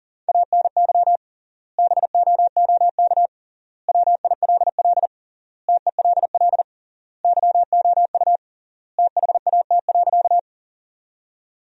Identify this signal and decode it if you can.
Morse code, heard as a keyed tone, about 40 wpm, 700 Hz. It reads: ANY BOOK WILL TELL YOU THAT.
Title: 3x Word-Spacing - Sentences from Top 300 Words - 40wpm